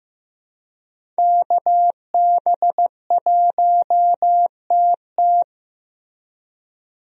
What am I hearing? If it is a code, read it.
Morse code, heard as a keyed tone, about 15 wpm, 700 Hz. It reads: KB1TT
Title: USA Call Signs - 5 Characters - Repeated 2x - 15wpm